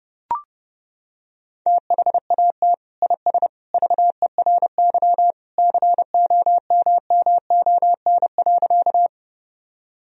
Morse code, heard as a keyed tone, about 30 wpm, 700 Hz. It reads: THAT IS VERY COMMON.
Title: Sentences from Top 1000 Words - Review Part 3 - 30wpm